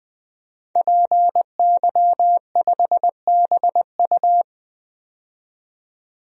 Morse code, heard as a keyed tone, about 20 wpm, 700 Hz. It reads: PY5BU
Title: DXCC-100 Call Signs - 5 Characters - 20wpm